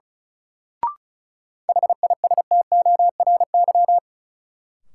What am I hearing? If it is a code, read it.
Morse code, heard as a keyed tone, about 36 wpm, 700 Hz. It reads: HISTORY
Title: Sentences from Top 600 Words - New Word Drill - 35wpm